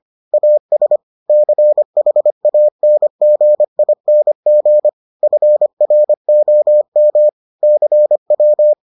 Morse code, heard as a keyed tone, about 25 wpm, 600 Hz. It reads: AS CHANGING FROM CW